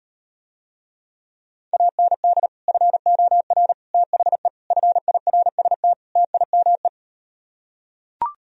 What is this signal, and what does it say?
Morse code, heard as a keyed tone, about 38 wpm, 700 Hz. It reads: AND FOR THE FIRST TIME